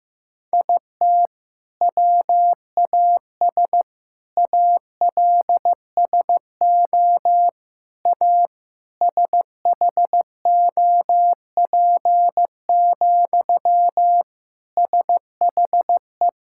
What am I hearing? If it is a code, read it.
Morse code, heard as a keyed tone, about 15 wpm, 700 Hz. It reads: IT WAS ALSO A SHOP, SHE
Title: Sentences from Top 1600 Words 15wpm